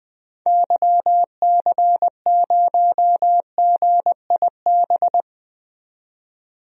Morse code, heard as a keyed tone, about 21 wpm, 700 Hz. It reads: YC0GIB
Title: DXCC-100 Call Signs - 6 Characters - 20wpm